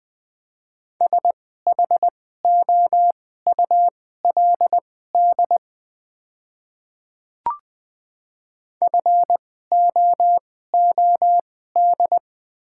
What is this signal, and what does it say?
Morse code, heard as a keyed tone, about 20 wpm, spaced as for 14 wpm, 700 Hz. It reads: SHOULD FOOD
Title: 2x Character Spacing - Top 300 Words - 20wpm